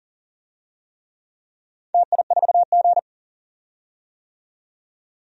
Morse code, heard as a keyed tone, about 41 wpm, 700 Hz. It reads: TI4G
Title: DXCC-100 Call Signs - 4 Characters - 40wpm